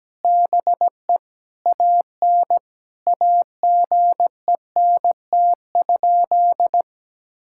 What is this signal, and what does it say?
Morse code, heard as a keyed tone, about 17 wpm, 700 Hz. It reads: BE AN AGENT?
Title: Sentences from Top 1600 Words 17wpm